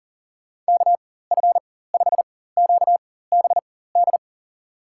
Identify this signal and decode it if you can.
Morse code, heard as a keyed tone, about 40 wpm, spaced as for 19 wpm, 700 Hz. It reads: KF5QBD
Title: USA Call Signs - 6 Characters - 4x Character Spacing - 40wpm